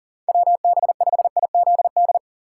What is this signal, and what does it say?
Morse code, heard as a keyed tone, about 40 wpm, 700 Hz. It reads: WB5IZD
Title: USA Call Signs - 2x Character Spacing - 40wpm